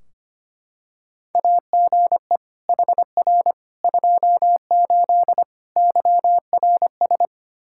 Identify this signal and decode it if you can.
Morse code, heard as a keyed tone, about 25 wpm, 700 Hz. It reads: AGE HR 28 YRS
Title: Practice QSO - Exchange 3 - More Info - 25wpm